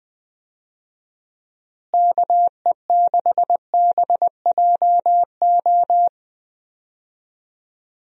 Morse code, heard as a keyed tone, about 20 wpm, 700 Hz. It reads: KE6BJO
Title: USA Call Signs - 6 Characters - Repeated 3x - 20wpm